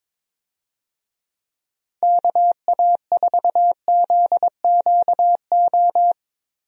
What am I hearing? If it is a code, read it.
Morse code, heard as a keyed tone, about 22 wpm, 700 Hz. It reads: KA4ZQO
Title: USA Call Signs - 6 Characters - Repeated 3x - 22wpm